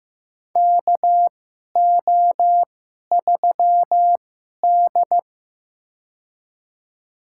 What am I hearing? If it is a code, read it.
Morse code, heard as a keyed tone, about 15 wpm, spaced as for 11 wpm, 700 Hz. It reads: KO3D
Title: USA Call Signs - 4 Characters - 2x Character Spacing - 15wpm